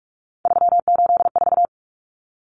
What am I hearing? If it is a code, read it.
Morse code, heard as a keyed tone, about 44 wpm, 700 Hz. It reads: 384
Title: Common Cut Numbers - 3-Digit 45wpm